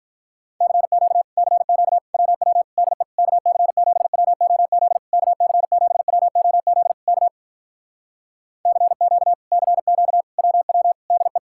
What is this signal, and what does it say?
Morse code, heard as a keyed tone, about 46 wpm, 700 Hz. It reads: CQ CQ WW DE KC7WOZ KC7WOZ K CQ CQ WW DE